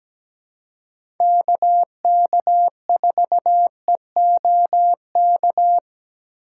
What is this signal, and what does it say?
Morse code, heard as a keyed tone, about 17 wpm, 700 Hz. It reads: KK4EOK
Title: USA Call Signs - 6 Characters - 6x Character Spacing - 17wpm